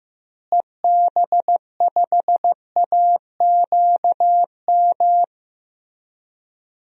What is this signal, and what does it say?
Morse code, heard as a keyed tone, about 15 wpm, 700 Hz. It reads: EB5AQM
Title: DXCC-100 Call Signs - 6 Characters - Repeated 3x - 15wpm